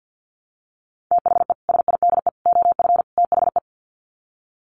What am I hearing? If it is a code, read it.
Morse code, heard as a keyed tone, about 50 wpm, 700 Hz. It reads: THE SIDE OF THE
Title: Sets of 4 Words 50wpm